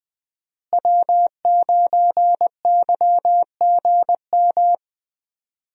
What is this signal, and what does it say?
Morse code, heard as a keyed tone, about 20 wpm, 700 Hz. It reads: W9YGM
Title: USA Call Signs - 5 Characters - Repeated 3x - 20wpm